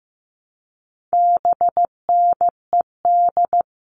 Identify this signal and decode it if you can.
Morse code, heard as a keyed tone, about 15 wpm, 700 Hz. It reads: BNED
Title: All Cut Numbers - 4-Digit 15wpm